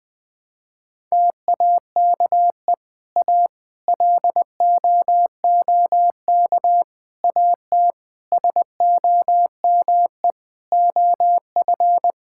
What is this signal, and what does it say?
Morse code, heard as a keyed tone, about 20 wpm, 700 Hz. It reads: TAKE A LOOK AT SOME OF
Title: Sentences with 7-Word Phrases - 20wpm